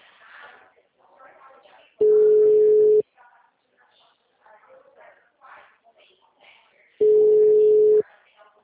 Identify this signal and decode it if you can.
Ringback tone (Europe, 425 Hz), with speech.
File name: out- -2014- - - .6983